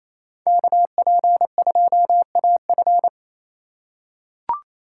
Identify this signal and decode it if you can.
Morse code, heard as a keyed tone, about 28 wpm, 700 Hz. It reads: KP2AF
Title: USA Call Signs - 5 Characters - 4x Character Spacing - 28wpm